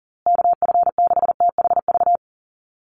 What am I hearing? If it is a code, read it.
Morse code, heard as a keyed tone, about 40 wpm, 700 Hz. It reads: KF6THV
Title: USA Call Signs Repeated 2x 40wpm